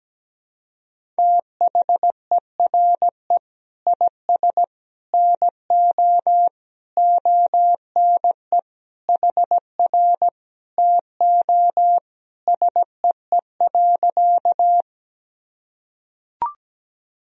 Morse code, heard as a keyed tone, about 17 wpm, 700 Hz. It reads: THERE IS NO ONE HR TO SEE.
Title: Sentences from Top 200 Words and Abbreviations 17wpm